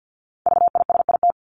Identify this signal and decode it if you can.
Morse code, heard as a keyed tone, about 50 wpm, 700 Hz. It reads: VISIT
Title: Top 751-1000 Words 50wpm